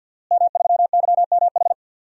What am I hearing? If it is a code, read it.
Morse code, heard as a keyed tone, about 46 wpm, 700 Hz. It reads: M3YMH
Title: DXCC-100 Call Signs - 5 Characters - Repeated 2x - 50wpm